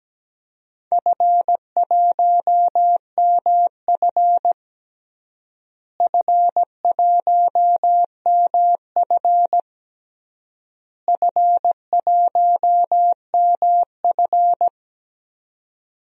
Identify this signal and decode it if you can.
Morse code, heard as a keyed tone, about 17 wpm, 700 Hz. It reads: F1MF F1MF F1MF
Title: DXCC-100 Call Signs - 4 Characters - Repeated 3x - 17wpm